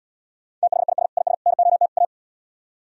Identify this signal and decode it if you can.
Morse code, heard as a keyed tone, about 77 wpm, 700 Hz. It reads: THIS IS NOT A